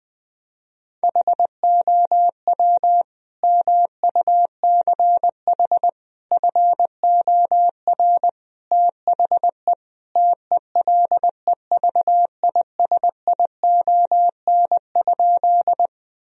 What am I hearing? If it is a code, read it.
Morse code, heard as a keyed tone, about 21 wpm, 700 Hz. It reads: HOW MUCH FOR THE TELEVISION?
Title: Sentences from Top 1000 Words - Review Part 4 - 20wpm